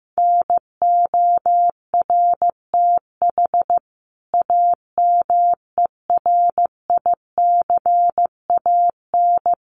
Morse code, heard as a keyed tone, about 15 wpm, 700 Hz. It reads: NORTH AMERICAN
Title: Sets of 5 Words Encore 15wpm